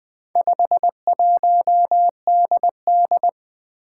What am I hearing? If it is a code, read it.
Morse code, heard as a keyed tone, about 20 wpm, 700 Hz. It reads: 51DD
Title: Four Challenging Letter-Number Combinations 20wpm